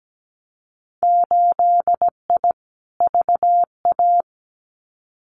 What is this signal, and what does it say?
Morse code, heard as a keyed tone, about 17 wpm, 700 Hz. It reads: 8I VA
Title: Winter Field Day Exchanges Encore 17wpm